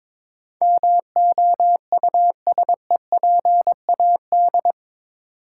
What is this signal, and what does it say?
Morse code, heard as a keyed tone, about 24 wpm, 700 Hz. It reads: MOUSEPAD
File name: Easy Words - Speed-Racing with Hints - 22wpm